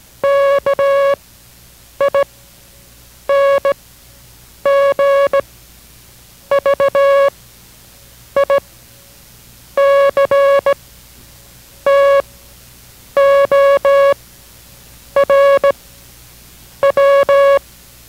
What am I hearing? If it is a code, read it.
Morse code, heard as a keyed tone, about 17 wpm, spaced as for 7 wpm, 560 Hz. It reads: KING VICTOR W